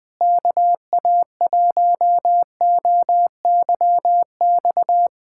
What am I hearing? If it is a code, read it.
Morse code, heard as a keyed tone, about 20 wpm, 700 Hz. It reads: KA1OYX
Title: USA Callsign DE Callsign - 6 Characters - 20wpm